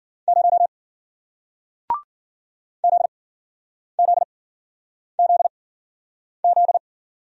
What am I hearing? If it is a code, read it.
Morse code, heard as a keyed tone, about 54 wpm, spaced as for 9 wpm, 700 Hz. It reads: 0 ZZZZ